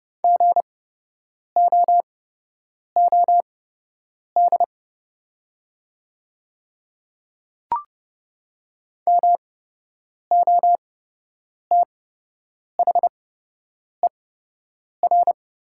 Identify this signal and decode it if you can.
Morse code, heard as a keyed tone, about 30 wpm, spaced as for 8 wpm, 700 Hz. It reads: GOOD MOTHER
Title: 8x Character Spacing - Top 300 Words - 30wpm